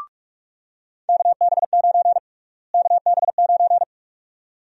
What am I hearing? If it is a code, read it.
Morse code, heard as a keyed tone, about 42 wpm, 700 Hz. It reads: KB9 KB9
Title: USA Call Sign Prefixes - Repeated 2x - 45wpm